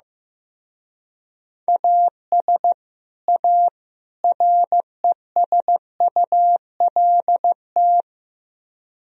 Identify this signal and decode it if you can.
Morse code, heard as a keyed tone, about 16 wpm, 700 Hz. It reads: AS A RESULT
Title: Speed-Racing - Sets of 3 Words 15wpm